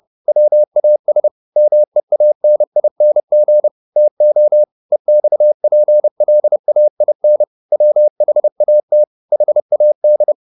Morse code, heard as a keyed tone, about 30 wpm, 600 Hz. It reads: WAS MEANING TO EXPLAIN WHAT HAD